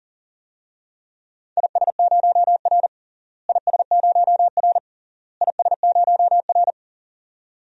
Morse code, heard as a keyed tone, about 37 wpm, 700 Hz. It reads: IS0R IS0R IS0R
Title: DXCC-100 Call Signs - 4 Characters - Repeated 3x - 40wpm